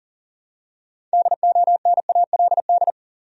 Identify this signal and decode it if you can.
Morse code, heard as a keyed tone, about 40 wpm, 700 Hz. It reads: DONALD
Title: USA Common Names - 40wpm